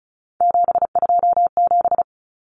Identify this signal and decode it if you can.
Morse code, heard as a keyed tone, about 34 wpm, 700 Hz. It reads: 727
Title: Numbers - 3-Digit 35wpm